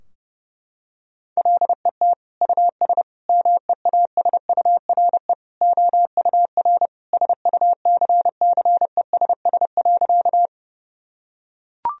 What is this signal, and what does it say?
Morse code, heard as a keyed tone, about 30 wpm, 700 Hz. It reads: LET US MEASURE OUR SUCCESS.
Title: Sentences from Top 1000 Words - Review Part 2 - 30wpm